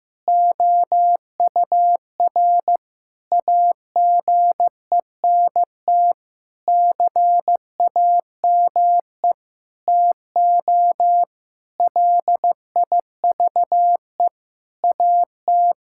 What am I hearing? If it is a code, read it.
Morse code, heard as a keyed tone, about 15 wpm, 700 Hz. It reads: OUR AGENT CAME TO LIVE AT
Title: Sentences from Top 1600 Words 15wpm